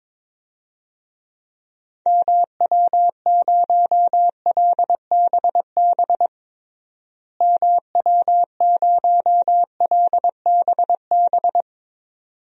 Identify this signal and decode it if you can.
Morse code, heard as a keyed tone, about 22 wpm, 700 Hz. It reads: MW0LBB MW0LBB
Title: DXCC-100 Call Signs - 6 Characters - Repeated 2x - 22wpm